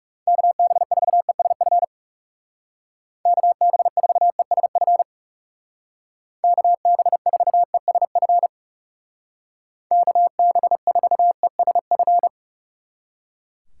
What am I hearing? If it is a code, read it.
Morse code, heard as a keyed tone, about 37 wpm, 700 Hz. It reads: KB4ESF KB4ESF KB4ESF KB4ESF
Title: Speed Racing - USA 2x3 Call Signs 30wpm